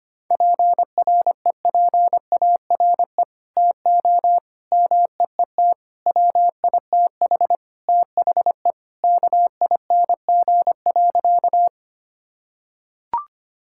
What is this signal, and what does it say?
Morse code, heard as a keyed tone, about 25 wpm, 700 Hz. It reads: PREPARE TO MEET WITH THE KING.
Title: Sentences from Top 1000 Words - Review Part 6 - 25wpm